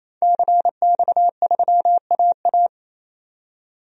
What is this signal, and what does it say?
Morse code, heard as a keyed tone, about 28 wpm, 700 Hz. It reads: CX3AA
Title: DXCC-100 Call Signs - 5 Characters - 28wpm